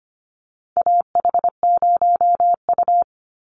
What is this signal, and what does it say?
Morse code, heard as a keyed tone, about 25 wpm, 700 Hz. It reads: AH0U
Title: Call Signs - 25wpm